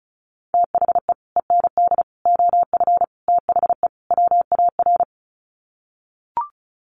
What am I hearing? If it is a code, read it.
Morse code, heard as a keyed tone, about 34 wpm, 700 Hz. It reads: THE END OF THE WAR
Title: Sets of 5 Words 35wpm